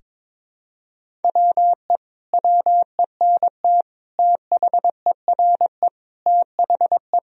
Morse code, heard as a keyed tone, about 22 wpm, 700 Hz. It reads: WE WENT THERE THE
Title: Sentences from Top 1000 Words - Review Part 4 - 22wpm